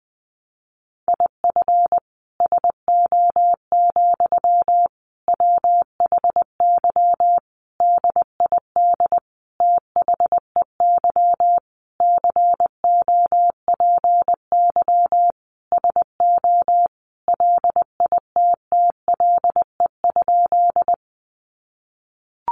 Morse code, heard as a keyed tone, about 20 wpm, 700 Hz. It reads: IF SO, WHY DID THEY COPY SO LITTLE?
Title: Sentences from Top 1400 Words 20wpm